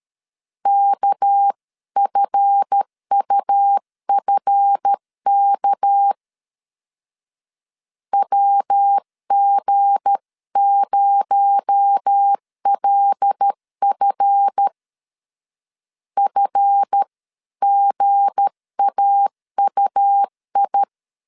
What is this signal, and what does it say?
Morse code, heard as a keyed tone, about 13 wpm, 800 Hz. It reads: KFUFK WG0LF FGAUI
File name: LEZIONI / 7 GF9Z / 7 GF 12wpm 2122